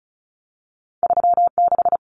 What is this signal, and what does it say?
Morse code, heard as a keyed tone, about 36 wpm, 700 Hz. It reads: 36
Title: Common Cut Numbers - 2-Digit 35wpm